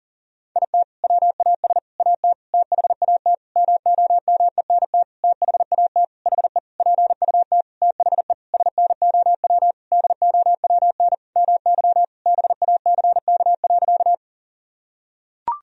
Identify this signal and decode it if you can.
Morse code, heard as a keyed tone, about 37 wpm, 700 Hz. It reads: IT WAS AT THAT MOMENT THAT HE PUT THE SNOW DOWN MY BACK.
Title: Sentences from Top 1000 Words - Review Part 3 - 40wpm